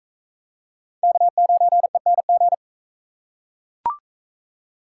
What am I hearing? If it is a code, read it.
Morse code, heard as a keyed tone, about 42 wpm, 700 Hz. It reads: K9ENG